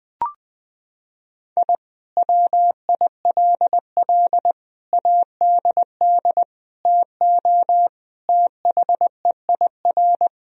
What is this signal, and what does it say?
Morse code, heard as a keyed tone, about 20 wpm, 700 Hz. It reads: I WILL ADD TO THEIR
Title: Sentences from Top 1000 Words - Review Part 1 - 20wpm